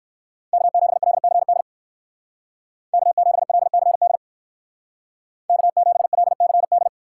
Morse code, heard as a keyed tone, about 59 wpm, 700 Hz. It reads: K7LCD K7LCD K7LCD